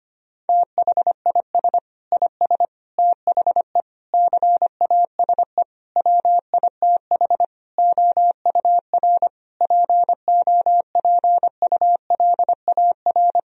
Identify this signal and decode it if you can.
Morse code, heard as a keyed tone, about 25 wpm, 700 Hz. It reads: THIS IS THE CASE WITH OUR POPULAR
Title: Sentences from Top 1000 Words - Review Part 3 - 25wpm